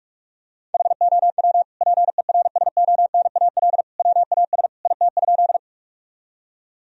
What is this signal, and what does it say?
Morse code, heard as a keyed tone, about 46 wpm, 700 Hz. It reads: HOW PERSONAL WAS IT?
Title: Sentences from Top 1000 Words - Review Part 4 - 45wpm